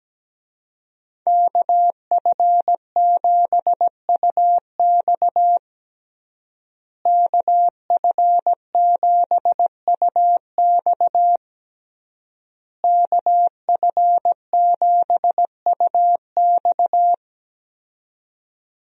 Morse code, heard as a keyed tone, about 17 wpm, 700 Hz. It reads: KF7UX KF7UX KF7UX
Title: USA Call Signs - 5 Characters - Repeated 3x - 17wpm